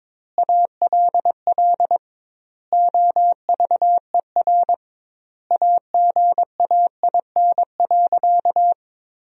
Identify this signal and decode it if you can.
Morse code, heard as a keyed tone, about 22 wpm, 700 Hz. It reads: ALL OVER AGAIN.
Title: 2x Word-Spacing - Sentences from Top 500 Words - 22wpm